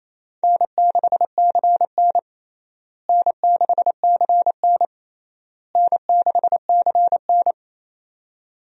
Morse code, heard as a keyed tone, about 28 wpm, 700 Hz. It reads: N6CN N6CN N6CN
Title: USA Call Signs - 4 Characters - Repeated 3x - 28wpm